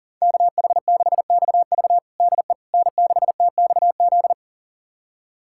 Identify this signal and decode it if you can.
Morse code, heard as a keyed tone, about 40 wpm, 700 Hz. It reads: KH6XV DE N6TXZ
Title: USA Callsign DE Callsign - 5 Characters - 40wpm